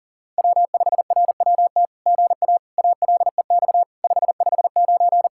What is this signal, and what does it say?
Morse code, heard as a keyed tone, about 40 wpm, 700 Hz. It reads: W5RWT GA ALEX 559
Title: SOTA QSO Practice 40wpm